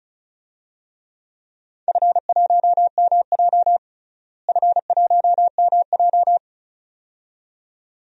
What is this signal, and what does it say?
Morse code, heard as a keyed tone, about 36 wpm, 700 Hz. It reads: F1MJ F1MJ
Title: DXCC-100 Call Signs - 4 Characters - Repeated 2x - 35wpm